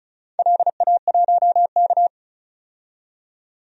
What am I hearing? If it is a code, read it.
Morse code, heard as a keyed tone, about 34 wpm, 700 Hz. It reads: LA1K
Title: DXCC-100 Call Signs - 4 Characters - 35wpm